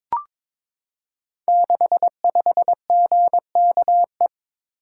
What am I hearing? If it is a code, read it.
Morse code, heard as a keyed tone, about 22 wpm, 700 Hz. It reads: 65GKE
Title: Code Groups - 5 Letter:Numbers - 22wpm